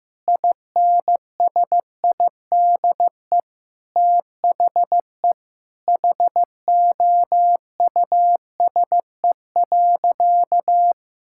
Morse code, heard as a keyed tone, about 15 wpm, 700 Hz. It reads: INSIDE THE HOUSE.
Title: Sentences from Top 1000 Words - Review Part 6 - 15wpm